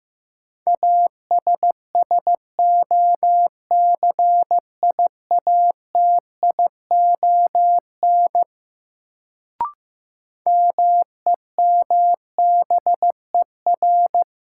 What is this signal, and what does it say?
Morse code, heard as a keyed tone, about 15 wpm, 700 Hz. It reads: ASSOCIATION MEMBER
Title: Sentences from Top 1100 Words - New Word Drill - 15wpm